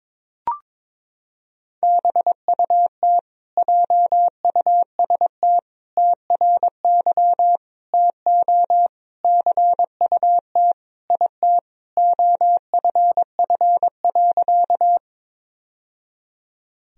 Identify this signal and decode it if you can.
Morse code, heard as a keyed tone, about 22 wpm, 700 Hz. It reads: BUT JUST TRY TO CUT IT OFF.
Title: Sentences from Top 1000 Words - Review Part 1 - 22wpm